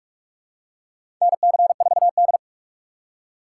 Morse code, heard as a keyed tone, about 44 wpm, 700 Hz. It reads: NC4D
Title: USA Call Signs - 6x Character Spacing - 45wpm